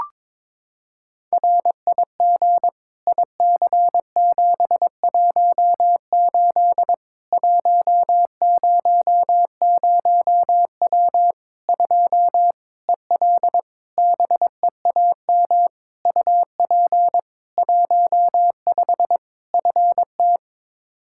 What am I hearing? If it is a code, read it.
Morse code, heard as a keyed tone, about 22 wpm, 700 Hz. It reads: RIG IC718 100W 2 EL BEAM UP 15 FT
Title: Practice QSO - Exchange 2 - Info - 22wpm